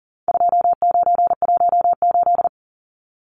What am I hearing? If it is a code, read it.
Morse code, heard as a keyed tone, about 40 wpm, 700 Hz. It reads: 2918
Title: Numbers - 4-Digit 40wpm